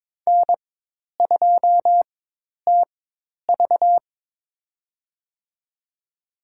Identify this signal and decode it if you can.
Morse code, heard as a keyed tone, about 22 wpm, spaced as for 10 wpm, 700 Hz. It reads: N2TV
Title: USA Call Signs - 4 Characters - 4x Character Spacing - 22wpm